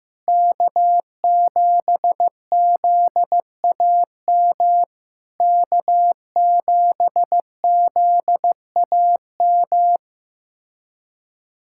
Morse code, heard as a keyed tone, about 15 wpm, 700 Hz. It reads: K7ZAM K7ZAM
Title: SOTA QSO Practice 15wpm